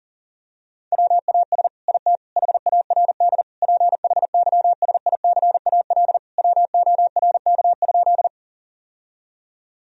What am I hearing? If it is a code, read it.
Morse code, heard as a keyed tone, about 40 wpm, 700 Hz. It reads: WAS IT HARD PHYSICAL WORK?
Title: Sentences from Top 1000 Words - Review Part 4 - 40wpm